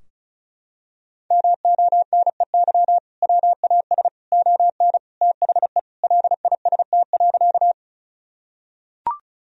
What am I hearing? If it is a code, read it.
Morse code, heard as a keyed tone, about 34 wpm, 700 Hz. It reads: MONEY WAS ON THE LIST.
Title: Sentences from Top 1000 Words - Review Part 2 - 35wpm